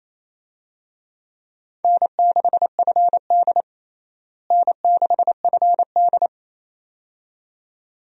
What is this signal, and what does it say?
Morse code, heard as a keyed tone, about 28 wpm, 700 Hz. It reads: N6FD N6FD
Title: USA Call Signs - 4 Characters - Repeated 2x - 28wpm